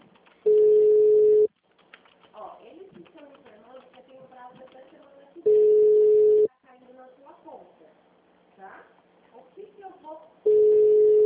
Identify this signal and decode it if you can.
Ringback tone (Europe, 425 Hz), with speech.